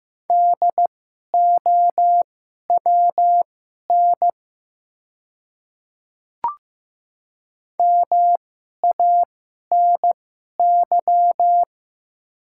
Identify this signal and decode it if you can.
Morse code, heard as a keyed tone, about 15 wpm, spaced as for 11 wpm, 700 Hz. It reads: DOWN MANY
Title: 2x Character Spacing - Top 100 Words - 15wpm